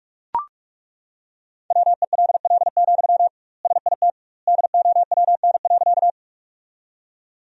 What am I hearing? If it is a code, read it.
Morse code, heard as a keyed tone, about 42 wpm, 700 Hz. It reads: WELL, SIT DOWN.
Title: 2x Word-Spacing - Sentences from Top 500 Words - 45wpm